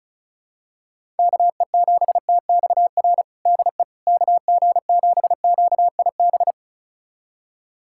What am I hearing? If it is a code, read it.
Morse code, heard as a keyed tone, about 36 wpm, 700 Hz. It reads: KE7TXR DE KG7QIB
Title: USA Callsign DE Callsign - 6 Characters - 35wpm